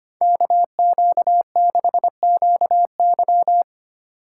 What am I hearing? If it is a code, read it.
Morse code, heard as a keyed tone, about 30 wpm, 700 Hz. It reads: KQ6QY
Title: Speed Racing - USA 2x2 Call Signs 25wpm